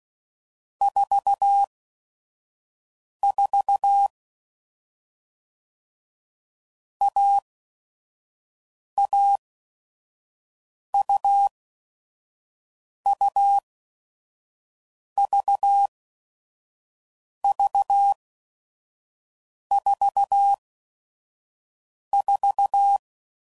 Morse code, heard as a keyed tone, about 16 wpm, spaced as for 5 wpm, 800 Hz. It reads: 44 AAUUVV44
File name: Mixed / Short Runs / AUV AUV4